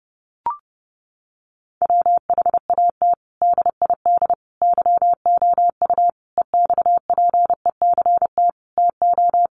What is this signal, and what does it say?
Morse code, heard as a keyed tone, about 30 wpm, 700 Hz. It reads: WHAT DID YOU EXPECT TO
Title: Sentences from Top 1100 Words 30wpm